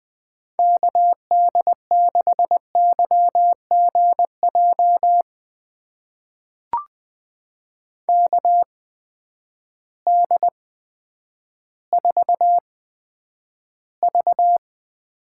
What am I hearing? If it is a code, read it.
Morse code, heard as a keyed tone, about 21 wpm, spaced as for 10 wpm, 700 Hz. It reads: KD6YGJ KD4V
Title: USA Call Signs - 8x Character Spacing - 20wpm